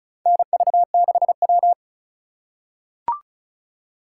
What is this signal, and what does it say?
Morse code, heard as a keyed tone, about 36 wpm, 700 Hz. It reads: NV6W
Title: USA Call Signs - 4 Characters - 3x Character Spacing - 35wpm